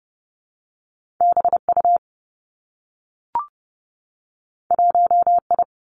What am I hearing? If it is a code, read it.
Morse code, heard as a keyed tone, about 30 wpm, 700 Hz. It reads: B U 1 I
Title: Two Challenging Letter-Number Combinations 30wpm